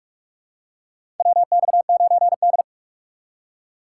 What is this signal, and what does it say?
Morse code, heard as a keyed tone, about 45 wpm, 700 Hz. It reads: WX9D
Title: USA Call Signs - 4x Character Spacing - 45wpm